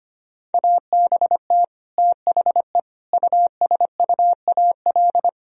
Morse code, heard as a keyed tone, about 25 wpm, 700 Hz. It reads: ABT THE USUAL
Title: Sentences from Top 400 Words and Abbreviations 25wpm